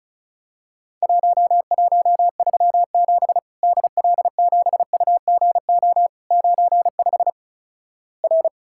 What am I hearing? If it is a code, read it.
Morse code, heard as a keyed tone, about 36 wpm, 700 Hz. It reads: 1137 DL7UGO 95 R
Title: Worked All Europe DX Contest - QTC Exchange - 35wpm